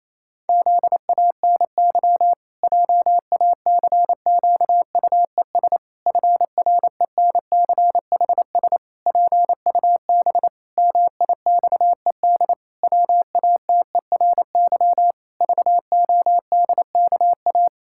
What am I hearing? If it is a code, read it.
Morse code, heard as a keyed tone, about 28 wpm, 700 Hz. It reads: ZANY JACQUES FRENCHS PUB MIXED WATERY VODKA
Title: Pangram Challenge v2 - 28wpm